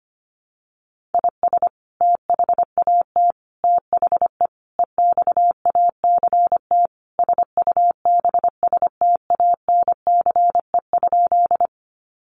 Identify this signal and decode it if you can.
Morse code, heard as a keyed tone, about 25 wpm, 700 Hz. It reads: IS THAT THE EXACT SUBSTANCE?
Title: Sentences from Top 1500 Words 25wpm